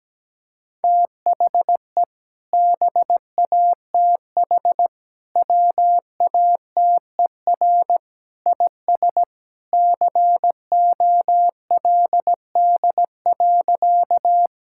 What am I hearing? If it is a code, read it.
Morse code, heard as a keyed tone, about 17 wpm, 700 Hz. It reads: THE BATH WATER IS COLD.
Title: Sentences from Top 1600 Words 17wpm